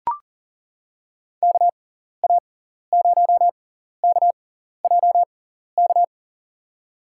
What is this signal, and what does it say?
Morse code, heard as a keyed tone, about 40 wpm, spaced as for 14 wpm, 700 Hz. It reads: KA0KJK
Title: USA Call Signs - 6 Characters - 6x Character Spacing - 40wpm